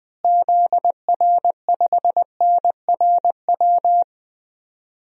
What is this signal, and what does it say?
Morse code, heard as a keyed tone, about 21 wpm, 700 Hz. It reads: ZR5NRW
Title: DXCC-100 Call Signs - 6 Characters - Repeated 3x - 20wpm